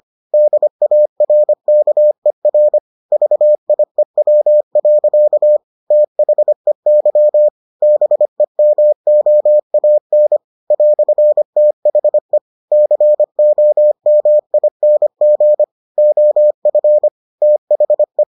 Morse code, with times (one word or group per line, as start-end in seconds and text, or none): 0.3-2.8 DARKER
3.1-5.6 VIEW.
5.9-7.5 THEY
7.8-10.4 BEMOAN
10.7-12.4 "THE
12.7-15.6 COMING
16.0-17.1 OF
17.4-18.2 THE